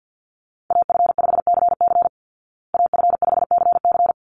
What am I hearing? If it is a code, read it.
Morse code, heard as a keyed tone, about 48 wpm, 700 Hz. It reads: AF5CC AF5CC
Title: Call Signs Repeated 3x - 50wpm